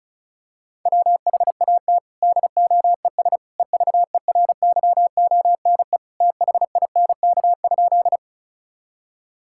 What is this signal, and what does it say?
Morse code, heard as a keyed tone, about 36 wpm, 700 Hz. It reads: WHAT DOES EVERYONE THINK?
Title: Sentences from Top 1000 Words - Review Part 5 - 35wpm